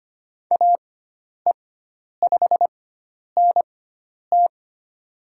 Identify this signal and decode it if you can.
Morse code, heard as a keyed tone, about 25 wpm, spaced as for 10 wpm, 700 Hz. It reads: AE5NT
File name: Lesson 16g - Letter c - Call Signs 25wpm